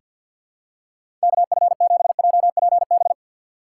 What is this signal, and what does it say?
Morse code, heard as a keyed tone, about 46 wpm, 700 Hz. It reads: KF7JPB
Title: USA Call Signs - 6 Characters - 4x Character Spacing - 50wpm